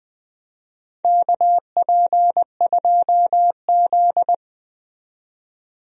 Morse code, heard as a keyed tone, about 20 wpm, 700 Hz. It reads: KP2Z
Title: DXCC-100 Call Signs - 4 Characters - 20wpm